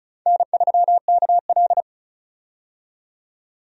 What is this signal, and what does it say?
Morse code, heard as a keyed tone, about 36 wpm, 700 Hz. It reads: N3KL
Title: USA Call Signs - 4 Characters - 35wpm